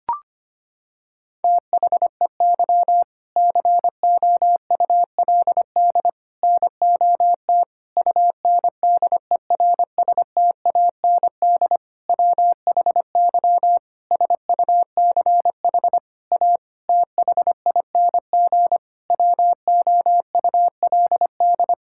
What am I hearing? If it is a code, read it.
Morse code, heard as a keyed tone, about 25 wpm, 700 Hz. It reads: THEY COULD NOT UNDERSTAND WHY SUCH A THING WOULD
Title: Sentences from Top 1000 Words - Review Part 6 - 25wpm